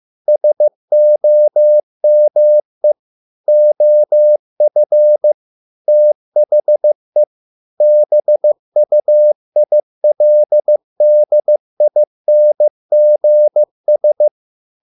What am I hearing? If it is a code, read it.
Morse code, heard as a keyed tone, about 15 wpm, 600 Hz. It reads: SOME OF THE BUILDINGS